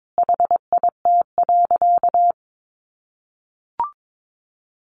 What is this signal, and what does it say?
Morse code, heard as a keyed tone, about 22 wpm, 700 Hz. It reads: HIT.
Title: Sentences from Top 1200 Words 22wpm